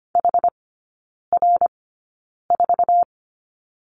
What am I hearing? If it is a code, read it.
Morse code, heard as a keyed tone, about 25 wpm, spaced as for 9 wpm, 700 Hz. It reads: HR4
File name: Lesson 14e - Number 2 - Three Characters 25wpm